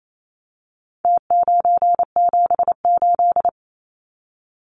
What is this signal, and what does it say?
Morse code, heard as a keyed tone, about 28 wpm, 700 Hz. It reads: T978
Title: Numbers - Contest Style - 4-Digit 28wpm